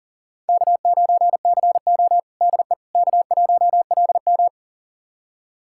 Morse code, heard as a keyed tone, about 37 wpm, 700 Hz. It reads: K9CO DE K1LM
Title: USA Callsign DE Callsign - 4 Characters - 40wpm